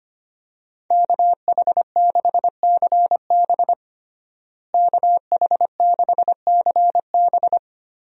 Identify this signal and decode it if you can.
Morse code, heard as a keyed tone, about 25 wpm, 700 Hz. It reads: KH6CB KH6CB
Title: USA Call Signs - 5 Characters - Repeated 2x - 25wpm